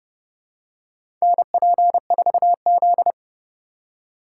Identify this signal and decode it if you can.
Morse code, heard as a keyed tone, about 30 wpm, 700 Hz. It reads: NP4Z
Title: USA Call Signs - 4 Characters - 3x Character Spacing - 30wpm